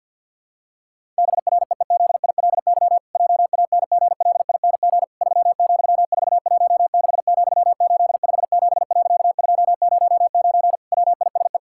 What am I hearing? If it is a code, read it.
Morse code, heard as a keyed tone, about 50 wpm, 700 Hz. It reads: BREEZILY JANGLING 3,416,857.209 WISE
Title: Pangram Challenge v2 - 50wpm